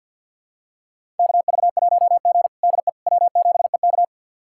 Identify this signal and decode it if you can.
Morse code, heard as a keyed tone, about 53 wpm, 700 Hz. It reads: KV1G DE W7EK
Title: USA Callsign DE Callsign - 4 Characters - 50wpm